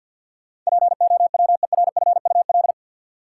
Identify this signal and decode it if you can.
Morse code, heard as a keyed tone, about 50 wpm, 700 Hz. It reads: POWERFUL
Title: Sets of 5 Words Encore - New Word Drill - 50wpm